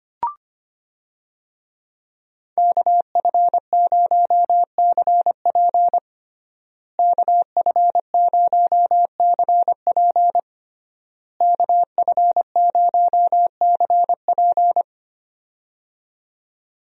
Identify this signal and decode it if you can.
Morse code, heard as a keyed tone, about 25 wpm, 700 Hz. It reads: KF0CP KF0CP KF0CP
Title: USA Call Signs - 5 Characters - Repeated 3x - 25wpm